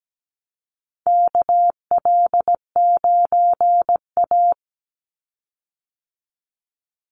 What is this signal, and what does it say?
Morse code, heard as a keyed tone, about 17 wpm, 700 Hz. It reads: KL9A
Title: Call Signs Repeated 3x - 17wpm